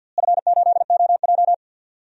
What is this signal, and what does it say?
Morse code, heard as a keyed tone, about 48 wpm, 700 Hz. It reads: F8OJ
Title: DXCC-100 Call Signs - 4 Characters - Repeated 3x - 50wpm